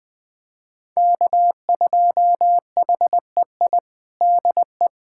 Morse code, heard as a keyed tone, about 20 wpm, 700 Hz. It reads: K2HEI DE
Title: USA Callsign DE Callsign - 5 Characters - 20wpm